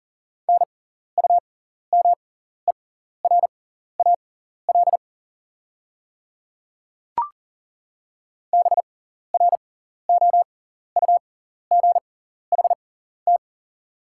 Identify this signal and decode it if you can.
Morse code, heard as a keyed tone, about 40 wpm, spaced as for 14 wpm, 690 Hz. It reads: NUMERAL BROUGHT
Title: 6x Character Spacing - Top 500 Words - 40wpm